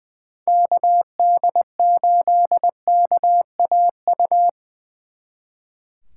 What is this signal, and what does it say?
Morse code, heard as a keyed tone, about 22 wpm, 700 Hz. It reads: KD8KAU
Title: Speed Racing - USA 2x3 Call Signs 20wpm